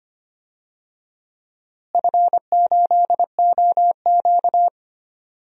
Morse code, heard as a keyed tone, about 25 wpm, 700 Hz. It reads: F8OQ
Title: DXCC-100 Call Signs - 4 Characters - Repeated 3x - 25wpm